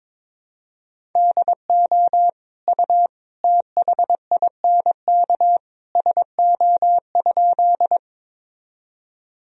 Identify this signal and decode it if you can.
Morse code, heard as a keyed tone, about 21 wpm, 700 Hz. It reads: DO U THINK SO?
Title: Sentences from Top 200 Words and Abbreviations 22wpm